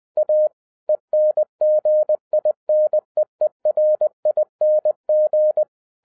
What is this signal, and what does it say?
Morse code, heard as a keyed tone, about 20 wpm, 610 Hz. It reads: A ENGINEERING
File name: 2011-07-mp3 / n23 QRP OTAKU 20wpm